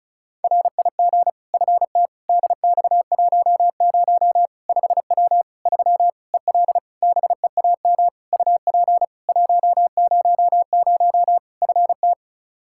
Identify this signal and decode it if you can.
Morse code, heard as a keyed tone, about 36 wpm, 700 Hz. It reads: RIG FT DX10 5W 3 EL BEAM UP 100 FT
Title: Practice QSO - Exchange 2 - Info - 35wpm